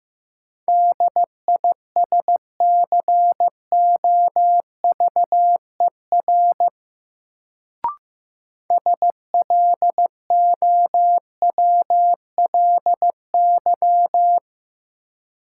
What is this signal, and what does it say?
Morse code, heard as a keyed tone, about 15 wpm, 700 Hz. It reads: DISCOVER SLOWLY
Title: Sentences from Top 1100 Words - New Word Drill Alt - 15wpm